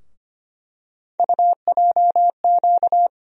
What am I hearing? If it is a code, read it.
Morse code, heard as a keyed tone, about 25 wpm, 700 Hz. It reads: UJQ
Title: USA Call Sign Postfixes - 3 Letters 25wpm